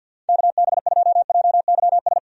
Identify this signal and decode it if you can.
Morse code, heard as a keyed tone, about 48 wpm, 700 Hz. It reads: KB2JYS
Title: USA Call Signs - 6 Characters - 50wpm